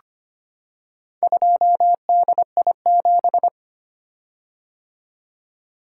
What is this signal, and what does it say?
Morse code, heard as a keyed tone, about 25 wpm, 700 Hz. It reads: 2DI7
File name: practice / 25wpm / Four Letter-Number Combinations - 25wpm